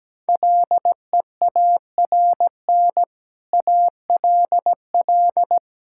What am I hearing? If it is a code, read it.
Morse code, heard as a keyed tone, about 17 wpm, 700 Hz. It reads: LEARN ALL
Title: Sentences from Top 1000 Words - Review Part 5 - 17wpm